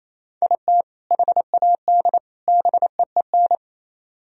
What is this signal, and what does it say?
Morse code, heard as a keyed tone, about 33 wpm, 700 Hz. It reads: IT HAD BEEN
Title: Speed-Racing - Sets of 3 Words 28wpm